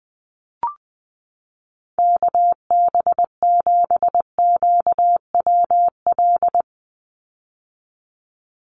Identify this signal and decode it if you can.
Morse code, heard as a keyed tone, about 20 wpm, 700 Hz. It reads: KB7QWL
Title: USA Call Signs 20wpm